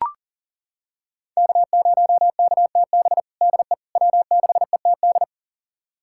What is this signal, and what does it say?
Morse code, heard as a keyed tone, about 40 wpm, 700 Hz. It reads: K0KTB DE W6ETD
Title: USA Callsign DE Callsign - 5 Characters - 40wpm